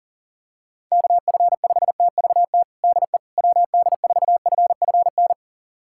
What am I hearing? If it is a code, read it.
Morse code, heard as a keyed tone, about 40 wpm, 700 Hz. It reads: KF5TVT DE WD4FFN
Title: USA Callsign DE Callsign - 6 Characters - 40wpm